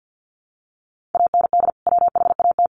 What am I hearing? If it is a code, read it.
Morse code, heard as a keyed tone, about 50 wpm, 700 Hz. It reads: AND WHAT
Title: Sentences from Top 1400 Words 50wpm